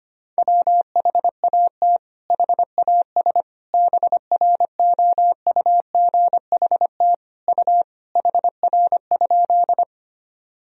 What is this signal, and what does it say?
Morse code, heard as a keyed tone, about 25 wpm, 700 Hz. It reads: WHAT HAS BROUGHT U HR?
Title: Sentences from Top 500 Words and Abbreviations 25wpm